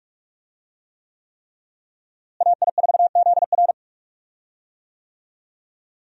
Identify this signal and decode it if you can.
Morse code, heard as a keyed tone, about 47 wpm, 700 Hz. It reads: AI4ZR
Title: USA Call Signs - 5 Characters - Repeated 3x - 45wpm